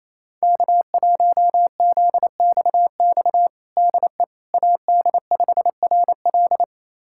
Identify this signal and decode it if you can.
Morse code, heard as a keyed tone, about 28 wpm, 700 Hz. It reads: K1ZXX DE AD5RL
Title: USA Callsign DE Callsign - 5 Characters - 28wpm